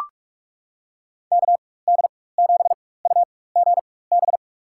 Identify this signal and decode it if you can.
Morse code, heard as a keyed tone, about 43 wpm, spaced as for 21 wpm, 700 Hz. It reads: KD7UGB
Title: USA Call Signs - 4x Character Spacing - 45wpm